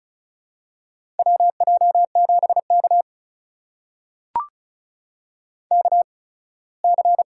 Morse code, heard as a keyed tone, about 36 wpm, spaced as for 21 wpm, 700 Hz. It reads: WJ7K KC
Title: USA Call Signs - 4 Characters - 8x Character Spacing - 35wpm